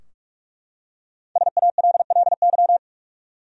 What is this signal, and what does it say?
Morse code, heard as a keyed tone, about 43 wpm, 700 Hz. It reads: SALLY
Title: USA Common Names - Max 5 Letters - 45wpm